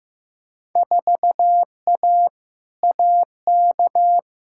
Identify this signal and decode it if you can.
Morse code, heard as a keyed tone, about 15 wpm, 700 Hz. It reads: 4A AK
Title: Field Day Exchanges - 15wpm